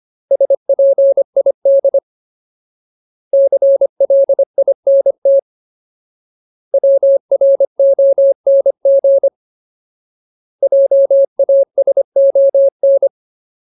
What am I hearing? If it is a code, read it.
Morse code, heard as a keyed tone, about 25 wpm, 550 Hz. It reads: SPID CLINT WRONG JASON
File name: Ultimate Word List / 25WPM Ultimate